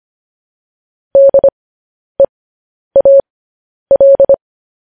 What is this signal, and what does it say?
Morse code, heard as a keyed tone, about 25 wpm, spaced as for 10 wpm, 550 Hz. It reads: DEAL